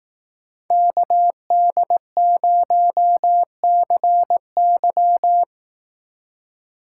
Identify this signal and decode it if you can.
Morse code, heard as a keyed tone, about 18 wpm, 700 Hz. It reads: KD0CY
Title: Speed Racing - USA 2x2 Call Signs 15wpm